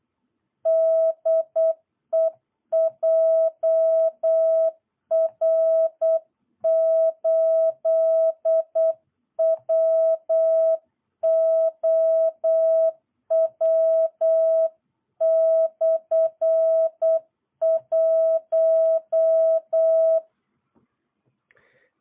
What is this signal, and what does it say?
Morse code, heard as a keyed tone, about 8 wpm, 660 Hz. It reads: DE JR8WOW/1
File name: jh8cbh / 2022cw